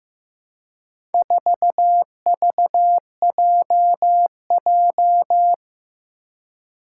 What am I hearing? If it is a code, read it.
Morse code, heard as a keyed tone, about 15 wpm, 700 Hz. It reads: 4VJJ
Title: Four Challenging Letter-Number Combinations 15wpm